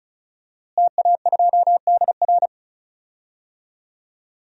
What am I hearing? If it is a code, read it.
Morse code, heard as a keyed tone, about 36 wpm, 700 Hz. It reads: TA2DR
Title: DXCC-100 Call Signs - 5 Characters - Repeated 2x - 35wpm